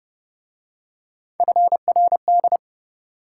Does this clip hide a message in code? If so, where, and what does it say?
Yes: Morse code from 1.4 to 2.6 s, heard as a keyed tone, about 31 wpm, 700 Hz. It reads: FRD